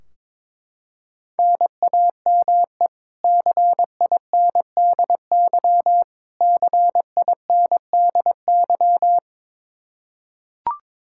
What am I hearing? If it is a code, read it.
Morse code, heard as a keyed tone, about 21 wpm, 700 Hz. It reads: NAME CINDY CINDY
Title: K1USN SST QSO Practice 22wpm